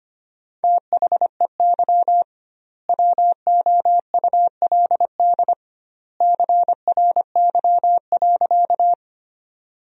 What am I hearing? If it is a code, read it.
Morse code, heard as a keyed tone, about 25 wpm, 700 Hz. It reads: THEY WOULD CRY.
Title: 2x Word-Spacing - Sentences from Top 500 Words - 25wpm